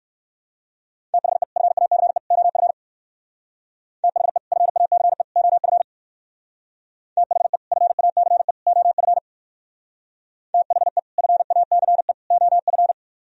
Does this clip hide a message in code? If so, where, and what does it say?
Yes: Morse code from 1.1 to 12.9 s, heard as a keyed tone, about 63 wpm, 700 Hz. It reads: THE FACE OF THE FACE OF THE FACE OF THE FACE OF